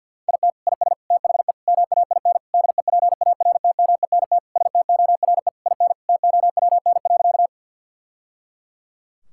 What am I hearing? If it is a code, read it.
Morse code, heard as a keyed tone, about 50 wpm, 700 Hz. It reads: IT IS THE MAIN DEPARTMENT STORE IN TOWN.
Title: Sentences from Top 1000 Words - Review Part 5 - 50wpm